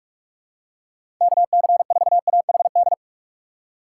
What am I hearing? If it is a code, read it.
Morse code, heard as a keyed tone, about 45 wpm, 700 Hz. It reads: KC4AHD
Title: USA Call Signs - 6 Characters - 2x Character Spacing - 45wpm